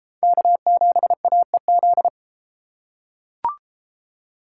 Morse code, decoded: K7AEZ

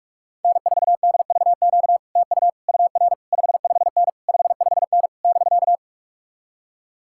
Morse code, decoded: N4DVQ TU UR 55N 55N <BK>

